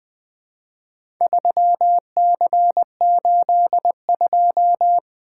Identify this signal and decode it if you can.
Morse code, heard as a keyed tone, about 20 wpm, 700 Hz. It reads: 3C82
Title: Four Letter-Number Combinations - 20wpm